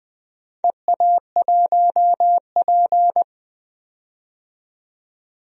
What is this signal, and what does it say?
Morse code, heard as a keyed tone, about 21 wpm, 700 Hz. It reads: EA1P
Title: DXCC-100 Call Signs - 4 Characters - 20wpm